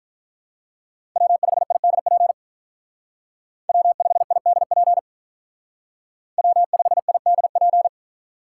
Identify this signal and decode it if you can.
Morse code, heard as a keyed tone, about 45 wpm, 700 Hz. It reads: W5IDP W5IDP W5IDP